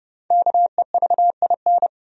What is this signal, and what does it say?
Morse code, heard as a keyed tone, about 30 wpm, 700 Hz. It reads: KEVIN
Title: USA Common Names - 30wpm